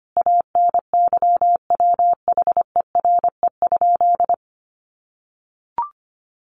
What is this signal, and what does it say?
Morse code, heard as a keyed tone, about 25 wpm, 700 Hz. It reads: ANYWHERE?
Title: Sentences from Top 1400 Words 25wpm